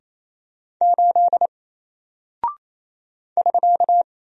Morse code, heard as a keyed tone, about 28 wpm, 700 Hz. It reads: 8 <SK>
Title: Single Character (Letter:Number:Punctuation:Prosign) Alt 28wpm